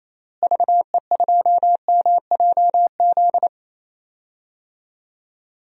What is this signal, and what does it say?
Morse code, heard as a keyed tone, about 28 wpm, 700 Hz. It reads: VE2MJZ
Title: DXCC-100 Call Signs - 6 Characters - Repeated 2x - 28wpm